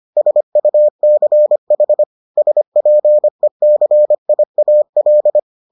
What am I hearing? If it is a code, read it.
Morse code, heard as a keyed tone, about 25 wpm, 600 Hz. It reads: SUCH SPECIAL